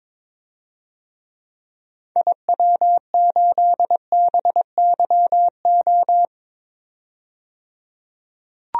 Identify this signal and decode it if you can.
Morse code, heard as a keyed tone, about 22 wpm, 700 Hz. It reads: IW8BYO
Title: DXCC-100 Call Signs - 6 Characters - 22wpm